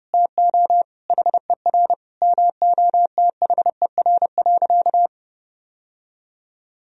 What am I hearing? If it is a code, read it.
Morse code, heard as a keyed tone, about 30 wpm, 700 Hz. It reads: TO HER MOTHER.
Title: Sentences from Top 1000 Words - Review Part 4 - 30wpm